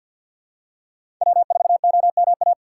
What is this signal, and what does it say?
Morse code, heard as a keyed tone, about 53 wpm, 690 Hz. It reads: W4OMA
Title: USA Call Signs - 5 Characters - 3x Character Spacing - 50wpm